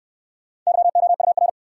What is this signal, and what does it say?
Morse code, heard as a keyed tone, about 61 wpm, 700 Hz. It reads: YOUR